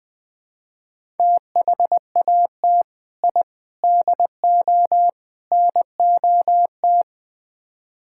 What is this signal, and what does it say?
Morse code, heard as a keyed tone, about 20 wpm, 700 Hz. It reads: THAT I DO NOT
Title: Build A Phrase - 4-Word Phrases - 20wpm